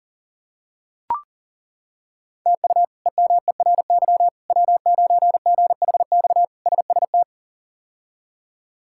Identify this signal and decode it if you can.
Morse code, heard as a keyed tone, about 37 wpm, 700 Hz. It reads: TU EMERY W9GHX SST
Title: K1USN SST QSO Practice - 40wpm